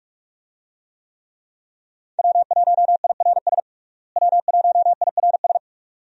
Morse code, heard as a keyed tone, about 41 wpm, 700 Hz. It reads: W1IRS W1IRS
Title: USA Call Signs - 5 Characters - Repeated 2x - 45wpm